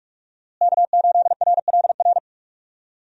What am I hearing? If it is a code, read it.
Morse code, heard as a keyed tone, about 46 wpm, 700 Hz. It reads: K8RLR
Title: USA Call Signs - 5 Characters - Repeated 3x - 45wpm